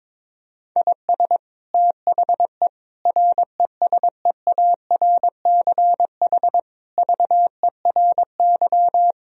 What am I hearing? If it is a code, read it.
Morse code, heard as a keyed tone, about 22 wpm, 700 Hz. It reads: IS THE RESEARCH VERY
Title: Sentences from Top 1000 Words - Review Part 4 - 22wpm